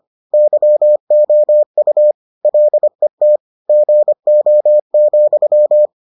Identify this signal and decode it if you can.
Morse code, heard as a keyed tone, about 25 wpm, 600 Hz. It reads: YOU LET GO,